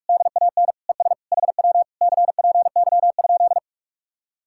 Morse code, heard as a keyed tone, about 58 wpm, 700 Hz. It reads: DAN ES HW CPY?